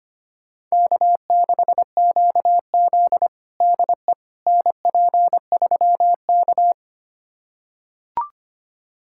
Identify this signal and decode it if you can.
Morse code, heard as a keyed tone, about 25 wpm, 700 Hz. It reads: K6QZ DE NP3K
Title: USA Callsign DE Callsign - 4 Characters - 25wpm